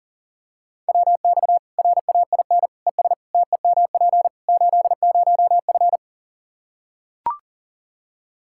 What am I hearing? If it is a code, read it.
Morse code, heard as a keyed tone, about 40 wpm, 700 Hz. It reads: WX RAIN ES TEMP 80F
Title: Practice QSO - Exchange 2 - Info - 40wpm